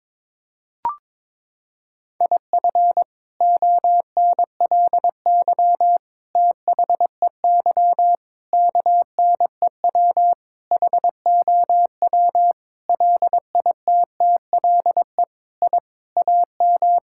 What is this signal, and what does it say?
Morse code, heard as a keyed tone, about 22 wpm, 700 Hz. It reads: IF ONLY THEY KNEW HOW LITTLE I AM
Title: Sentences from Top 1000 Words - Review Part 6 - 22wpm